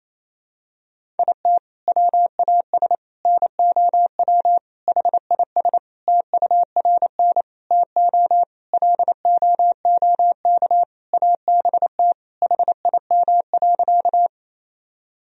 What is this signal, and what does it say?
Morse code, heard as a keyed tone, about 28 wpm, 700 Hz. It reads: IT WAS NOW HIS TURN TO LOOK ABT HIM.
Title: Sentences from Top 200 Words and Abbreviations 28wpm